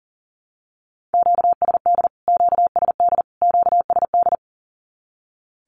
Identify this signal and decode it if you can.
Morse code, heard as a keyed tone, about 40 wpm, 700 Hz. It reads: QSD QSD QSD
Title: Q-Codes Repeated 3x 40wpm